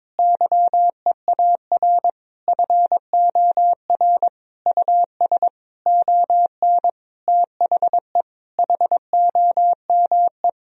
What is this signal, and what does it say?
Morse code, heard as a keyed tone, about 22 wpm, 700 Hz. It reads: YEAR FOR US ON THE HOME
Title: Sentences from Top 1000 Words - Review Part 2 - 22wpm